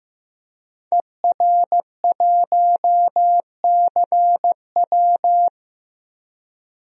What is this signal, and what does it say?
Morse code, heard as a keyed tone, about 15 wpm, 700 Hz. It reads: ER1CW
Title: DXCC-100 Call Signs - 5 Characters - Repeated 3x - 15wpm